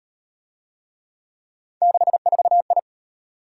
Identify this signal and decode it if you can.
Morse code, heard as a keyed tone, about 37 wpm, 700 Hz. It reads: 64I